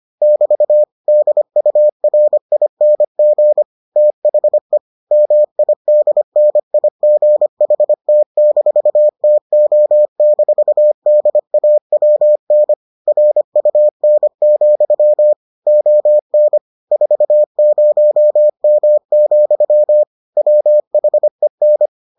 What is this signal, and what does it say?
Morse code, heard as a keyed tone, about 25 wpm, 600 Hz. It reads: <BT> DURING THE MIDNIGHT-TO-DAWN RUN, ON 40M, WHEN